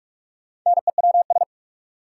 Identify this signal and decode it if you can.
Morse code, heard as a keyed tone, about 45 wpm, 700 Hz. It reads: NEWS
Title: Sets of 5 Words - New Word Drill - 45wpm